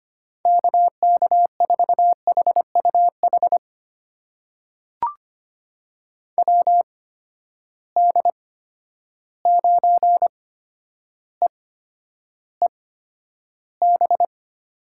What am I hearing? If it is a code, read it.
Morse code, heard as a keyed tone, about 25 wpm, spaced as for 11 wpm, 700 Hz. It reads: KK4HUH WD9EEB
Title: USA Call Signs - 6 Characters - 8x Character Spacing - 25wpm